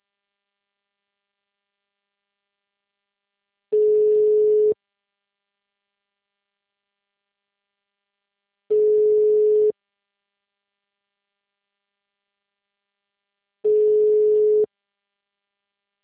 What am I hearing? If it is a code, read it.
Ringback tone (Europe, 425 Hz).
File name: out- -2030- - - .32207